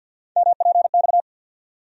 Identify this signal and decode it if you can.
Morse code, heard as a keyed tone, about 47 wpm, 700 Hz. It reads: MPX
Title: USA Call Sign Postfixes - 3 Letters 50wpm